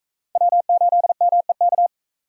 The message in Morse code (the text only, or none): W8MEK